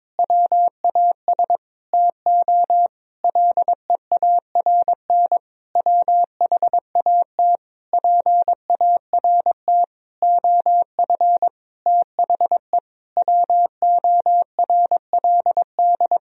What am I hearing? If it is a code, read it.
Morse code, heard as a keyed tone, about 23 wpm, 700 Hz. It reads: WAS TO LEARN WHAT PART OF THE WORLD
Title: Sentences from Top 1000 Words - Review Part 3 - 22wpm